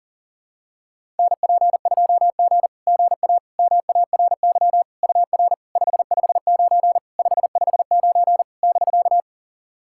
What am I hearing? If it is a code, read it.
Morse code, heard as a keyed tone, about 40 wpm, 700 Hz. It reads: NP2G GA MARY UR 559 559 <BK>
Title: SOTA QSO Practice 40wpm